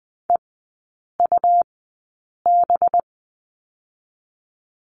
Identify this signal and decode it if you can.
Morse code, heard as a keyed tone, about 20 wpm, 700 Hz. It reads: E U B
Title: 218 Lesson 35e - Number 0 - Three Characters 20wpm